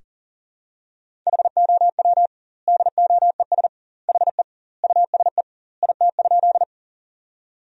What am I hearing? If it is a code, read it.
Morse code, heard as a keyed tone, about 40 wpm, 700 Hz. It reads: HOW DOES HE USE IT?
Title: 2x Word-Spacing - Sentences from Top 200 Words - 40wpm